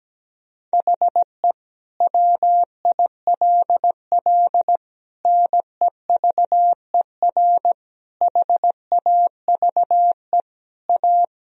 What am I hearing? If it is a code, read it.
Morse code, heard as a keyed tone, about 17 wpm, 700 Hz. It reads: HE WILL NEVER HAVE A
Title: Sentences from Top 1000 Words - Review Part 2 - 17wpm